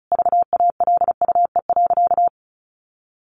Morse code, heard as a keyed tone, about 34 wpm, 700 Hz. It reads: VALUE.
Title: Sentences from Top 600 Words 35wpm